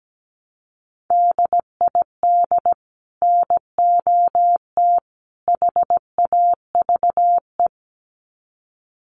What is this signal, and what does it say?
Morse code, heard as a keyed tone, about 17 wpm, 700 Hz. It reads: DID NOT HAVE
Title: Sets of 3 Words 17wpm